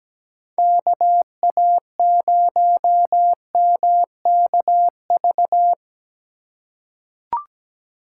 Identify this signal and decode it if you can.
Morse code, heard as a keyed tone, about 17 wpm, 700 Hz. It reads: KA0MKV
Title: USA Call Signs - 6 Characters - 2x Character Spacing - 17wpm